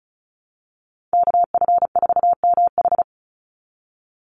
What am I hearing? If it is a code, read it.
Morse code, heard as a keyed tone, about 36 wpm, 700 Hz. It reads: KF4MH
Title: NA SOTA Chaser Callsigns 35wpm